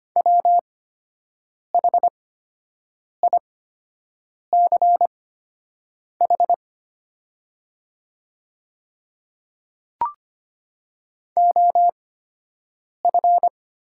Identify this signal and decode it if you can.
Morse code, heard as a keyed tone, about 25 wpm, spaced as for 7 wpm, 700 Hz. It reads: WHICH OF
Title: 8x Character Spacing - Top 100 Words - 25wpm